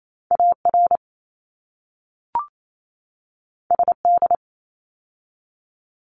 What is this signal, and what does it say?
Morse code, heard as a keyed tone, about 28 wpm, 700 Hz. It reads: AR SD
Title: ARRL Sections 28wpm